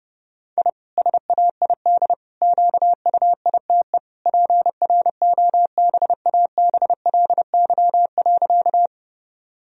Morse code, heard as a keyed tone, about 30 wpm, 700 Hz. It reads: I SAID QUITE PROBABLY.
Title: Sentences from Top 1000 Words - Review Part 4 - 30wpm